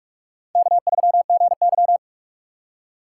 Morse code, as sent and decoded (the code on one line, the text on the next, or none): -.- ...-- --. -.--
K3GY